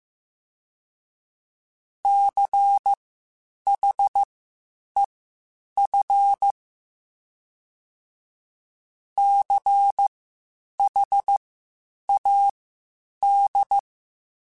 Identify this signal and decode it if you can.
Morse code, heard as a keyed tone, about 15 wpm, spaced as for 8 wpm, 800 Hz. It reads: CHEF CHAD
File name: Spaced Alphas / Spaced Alphas mp3s / Cs sp @13